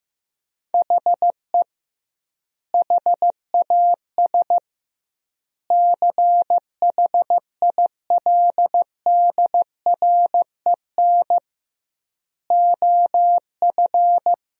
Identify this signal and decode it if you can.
Morse code, heard as a keyed tone, about 15 wpm, 700 Hz. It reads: HE HAS CHILDREN OF
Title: 2x Word-Spacing - Sentences from Top 300 Words - 15wpm